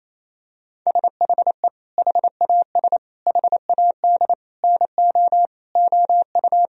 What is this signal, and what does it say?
Morse code, heard as a keyed tone, about 28 wpm, 700 Hz. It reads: SHE HAS HAD NO OU
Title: Sentences from Top 1000 Words - Review Part 4 - 28wpm